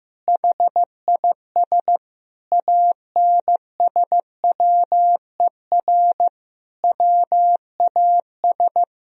Morse code, heard as a keyed tone, about 15 wpm, 700 Hz. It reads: HIS ANSWER WAS
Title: Sentences from Top 1000 Words - Review Part 2 - 15wpm